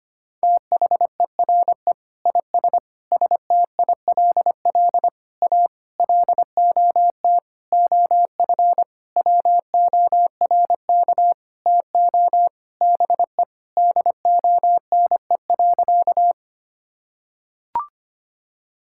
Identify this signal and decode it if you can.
Morse code, heard as a keyed tone, about 25 wpm, 700 Hz. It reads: THERE IS STILL A LOT OF WORK TO BE DONE.
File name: practice / 25wpm / Sentences with 9-Word Phrases - 25wpm